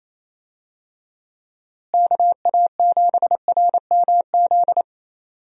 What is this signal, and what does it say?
Morse code, heard as a keyed tone, about 28 wpm, 700 Hz. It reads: KA7RMZ
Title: USA Call Signs - 6 Characters - Repeated 2x - 28wpm